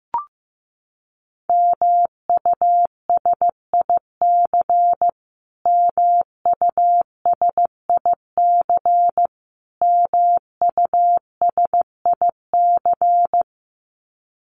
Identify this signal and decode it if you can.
Morse code, heard as a keyed tone, about 15 wpm, 700 Hz. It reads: MUSIC MUSIC MUSIC
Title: Top 500 Words Repeated 15wpm